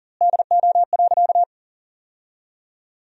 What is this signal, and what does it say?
Morse code, heard as a keyed tone, about 40 wpm, 700 Hz. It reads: DO.
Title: 2x Word-Spacing - Sentences from Top 100 Words - 40wpm